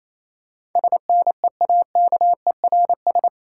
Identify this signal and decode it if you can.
Morse code, heard as a keyed tone, about 27 wpm, 690 Hz. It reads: SNEAKERS
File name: Easy Words - Speed-Racing with Hints - 22wpm